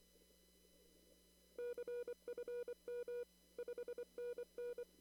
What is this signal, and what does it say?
Morse code, heard as a keyed tone, about 24 wpm, 480 Hz. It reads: CFM 5NN
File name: CW / kt5j wpx cw-op-k5pi